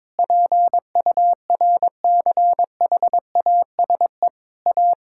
Morse code, heard as a keyed tone, about 22 wpm, 700 Hz. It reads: PURCHASE A
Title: Sentences from Top 1000 Words - Review Part 6 - 22wpm